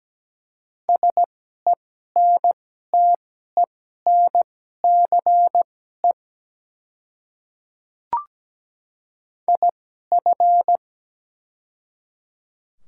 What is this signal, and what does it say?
Morse code, heard as a keyed tone, about 17 wpm, spaced as for 12 wpm, 700 Hz. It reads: SENTENCE IF
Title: 2x Character Spacing - Top 200 Words - 17wpm